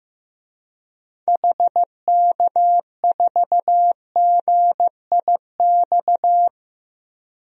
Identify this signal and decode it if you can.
Morse code, heard as a keyed tone, about 15 wpm, 700 Hz. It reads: HK4GIX
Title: DXCC-100 Call Signs - 6 Characters - 15wpm